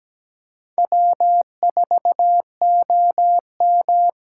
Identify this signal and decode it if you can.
Morse code, heard as a keyed tone, about 17 wpm, 700 Hz. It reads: W4OM
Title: USA Call Signs - 8x Character Spacing - 17wpm